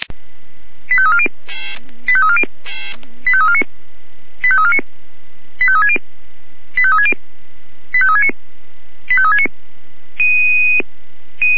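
Five-tone selective call (ZVEI), heard as quick runs of five tones, tones of about 70 ms. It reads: 85340 85340 85349 85349 85360 85360 85369 85369
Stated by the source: Five-tone ZVEI selective call received by SDR.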